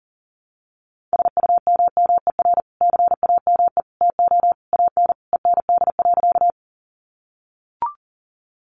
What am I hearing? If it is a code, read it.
Morse code, heard as a keyed tone, about 40 wpm, 700 Hz. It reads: SUMMER CAME TO AN END.
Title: Sentences from Top 1100 Words 40wpm